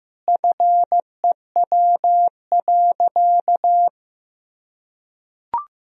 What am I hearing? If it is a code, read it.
Morse code, heard as a keyed tone, about 15 wpm, 700 Hz. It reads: FEW.
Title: Sentences from Top 1000 Words - Review Part 1 - 15wpm